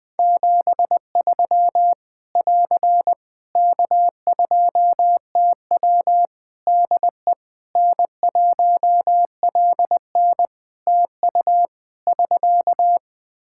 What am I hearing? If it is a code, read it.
Morse code, heard as a keyed tone, about 20 wpm, 700 Hz. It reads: 73 <AR> K2TW DE N1LN TU <SK>
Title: Practice QSO - Exchange 4 - Ending - 20wpm